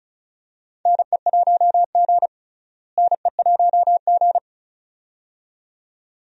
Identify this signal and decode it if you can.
Morse code, heard as a keyed tone, about 36 wpm, 700 Hz. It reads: NE1G NE1G
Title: USA Call Signs - 4 Characters - Repeated 2x - 35wpm